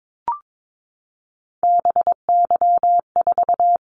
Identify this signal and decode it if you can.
Morse code, heard as a keyed tone, about 22 wpm, 700 Hz. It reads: BY4
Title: Callsign Prefixes for Top 100 DXCC - 22wpm